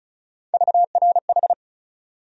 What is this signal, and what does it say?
Morse code, heard as a keyed tone, about 36 wpm, 700 Hz. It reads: VRH
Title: USA Call Sign Postfixes - 3 Letters 35wpm